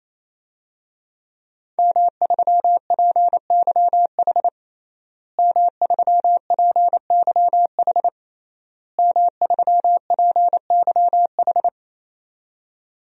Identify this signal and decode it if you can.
Morse code, heard as a keyed tone, about 28 wpm, 700 Hz. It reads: M3PYH M3PYH M3PYH
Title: DXCC-100 Call Signs - 5 Characters - Repeated 3x - 28wpm